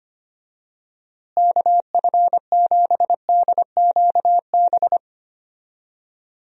Morse code, heard as a keyed tone, about 25 wpm, 700 Hz. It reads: KF7DQB
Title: USA Call Signs - 6 Characters - Repeated 2x - 25wpm